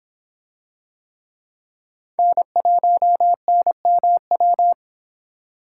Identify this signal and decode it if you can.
Morse code, heard as a keyed tone, about 23 wpm, 700 Hz. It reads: N1NMW